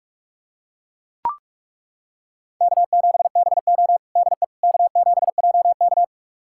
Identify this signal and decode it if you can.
Morse code, heard as a keyed tone, about 41 wpm, 700 Hz. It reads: K7BO DE K7JK
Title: USA Callsign DE Callsign - 4 Characters - 45wpm